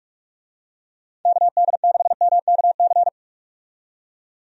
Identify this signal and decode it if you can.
Morse code, heard as a keyed tone, about 46 wpm, 700 Hz. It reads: KD6MKC
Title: USA Call Signs - 6 Characters - 45wpm